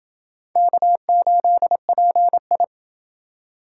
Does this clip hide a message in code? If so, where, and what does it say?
Yes: Morse code from 0.6 to 2.6 s, heard as a keyed tone, about 29 wpm, 700 Hz. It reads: K8PI